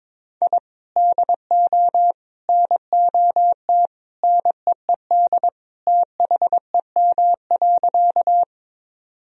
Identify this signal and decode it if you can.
Morse code, heard as a keyed tone, about 22 wpm, 700 Hz. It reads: I DO NOT NEED THEM.
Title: Sentences from Top 1000 Words - Review Part 1 - 22wpm